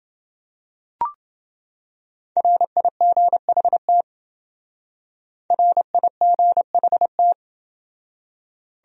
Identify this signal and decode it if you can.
Morse code, heard as a keyed tone, about 28 wpm, 700 Hz. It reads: RIGHT RIGHT